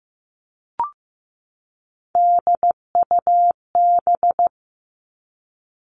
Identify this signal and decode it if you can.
Morse code, heard as a keyed tone, about 15 wpm, 700 Hz. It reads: DUB
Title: Three Challenging Letter-Number Combinations 15wpm